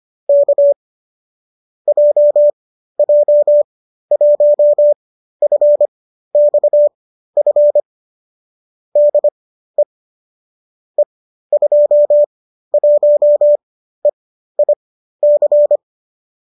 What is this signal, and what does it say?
Morse code, heard as a keyed tone, about 25 wpm, spaced as for 13 wpm, 570 Hz. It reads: K JJ1FXF DE E21EIC